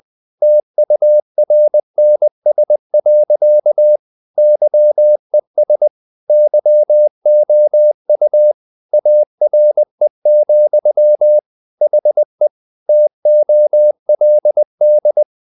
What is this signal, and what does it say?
Morse code, heard as a keyed tone, about 20 wpm, 600 Hz. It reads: TURNS. YES YOU ARE, HE TOLD
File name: the old man and the sea part 2 20wpm